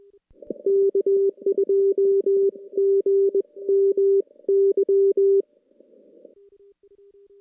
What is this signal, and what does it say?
Morse code, heard as a keyed tone, about 21 wpm, spaced as for 18 wpm, 390 Hz. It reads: K2GMY
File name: contests / CW-SS-k5tr- -2300